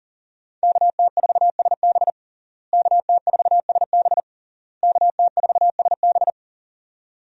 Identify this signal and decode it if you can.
Morse code, heard as a keyed tone, about 37 wpm, 700 Hz. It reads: KT4SB KT4SB KT4SB
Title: USA Call Signs - 5 Characters - Repeated 3x - 40wpm